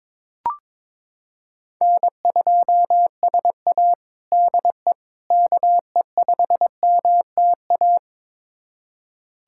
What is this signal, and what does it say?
Morse code, heard as a keyed tone, about 21 wpm, 700 Hz. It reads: N2SA DE KE5MTA
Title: USA Callsign DE Callsign 22wpm